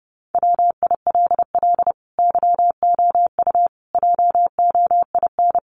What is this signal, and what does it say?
Morse code, heard as a keyed tone, about 28 wpm, 700 Hz. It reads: WILL YOU JOIN
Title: Sentences from Top 1400 Words 30wpm